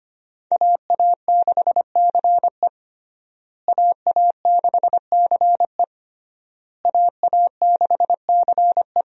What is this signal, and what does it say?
Morse code, heard as a keyed tone, about 25 wpm, 700 Hz. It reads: AA6CE AA6CE AA6CE
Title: USA Call Signs - 5 Characters - Repeated 3x - 25wpm